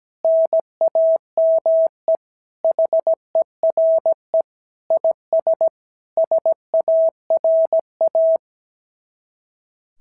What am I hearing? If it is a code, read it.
Morse code, heard as a keyed tone, about 17 wpm, 650 Hz. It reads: NAME HERE IS SARA
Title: SOTA QSO Practice 17wpm